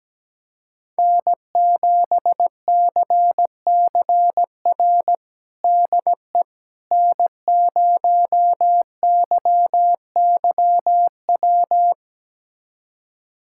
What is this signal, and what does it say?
Morse code, heard as a keyed tone, about 17 wpm, 700 Hz. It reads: N7CCR DE N0YYW
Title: USA Callsign DE Callsign - 5 Characters - 17wpm